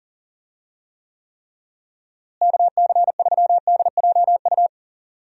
Morse code, heard as a keyed tone, about 40 wpm, 700 Hz. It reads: KC3DJU
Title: USA Call Signs - 6 Characters - Repeated 3x - 40wpm